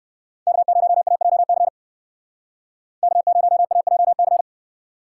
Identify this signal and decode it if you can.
Morse code, heard as a keyed tone, about 57 wpm, 700 Hz. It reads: K0AJC K0AJC